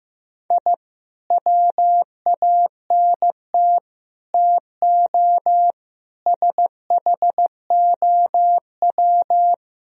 Morse code, heard as a keyed tone, about 15 wpm, 700 Hz. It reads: I WANT TO SHOW
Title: Speed-Racing - Sets of 5 Words 15wpm